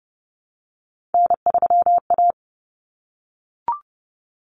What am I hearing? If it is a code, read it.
Morse code, heard as a keyed tone, about 30 wpm, 700 Hz. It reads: N3A
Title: Common Cut Numbers - 3-Digit 30wpm